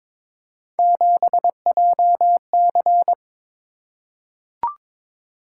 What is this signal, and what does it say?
Morse code, heard as a keyed tone, about 23 wpm, 700 Hz. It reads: 7JC
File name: practice / 22wpm / Three Letter-Number Combinations - 22wpm